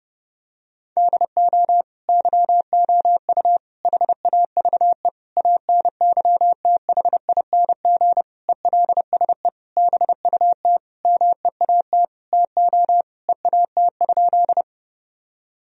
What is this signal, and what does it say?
Morse code, heard as a keyed tone, about 31 wpm, 700 Hz. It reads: DO YOU HAVE ANYTHING ELSE BUT MEAT TO EAT?
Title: Sentences from Top 1000 Words - Review Part 3 - 30wpm